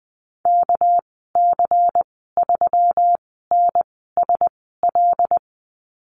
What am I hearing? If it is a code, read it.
Morse code, heard as a keyed tone, about 20 wpm, spaced as for 14 wpm, 700 Hz. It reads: KC3NSL
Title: 245 Lesson 39g - Letter z - Call Signs 20wpm